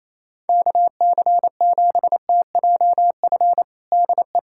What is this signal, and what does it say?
Morse code, heard as a keyed tone, about 28 wpm, 700 Hz. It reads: KC7TJF DE
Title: USA Callsign DE Callsign - 6 Characters - 28wpm